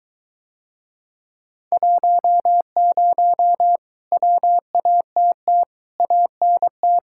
Morse code, heard as a keyed tone, about 23 wpm, 700 Hz. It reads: 10 WATT ANT